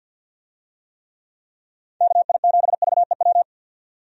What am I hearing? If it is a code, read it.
Morse code, heard as a keyed tone, about 52 wpm, 700 Hz. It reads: KI7VEW
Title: USA Call Signs - 6 Characters - 50wpm